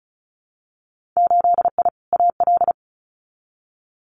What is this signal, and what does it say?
Morse code, heard as a keyed tone, about 36 wpm, 700 Hz. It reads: 8I AL
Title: Winter Field Day Exchanges Encore 35wpm